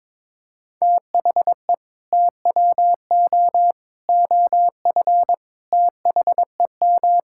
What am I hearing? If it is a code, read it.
Morse code, heard as a keyed tone, about 22 wpm, 700 Hz. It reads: THE TWO OF THEM
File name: practice / 22wpm / Sentences with 4-Word Phrases - 22wpm